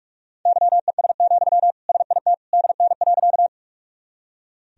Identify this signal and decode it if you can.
Morse code, heard as a keyed tone, about 41 wpm, 700 Hz. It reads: YES, SIT DN.
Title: Sentences from Top 500 Words and Abbreviations 45wpm